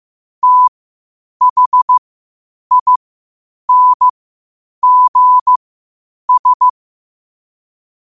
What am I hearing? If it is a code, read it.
Morse code, heard as a keyed tone, about 15 wpm, spaced as for 8 wpm, 1000 Hz. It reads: THINGS
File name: qotdfiles / 8wpm